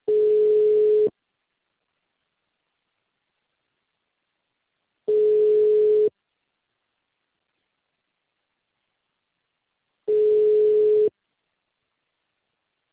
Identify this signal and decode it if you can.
Ringback tone (Europe, 425 Hz).
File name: out- -2019- - - .3611